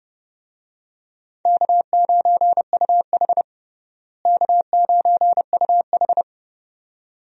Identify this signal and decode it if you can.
Morse code, heard as a keyed tone, about 32 wpm, 700 Hz. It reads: K9UH K9UH
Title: USA Call Signs - 4 Characters - Repeated 2x - 30wpm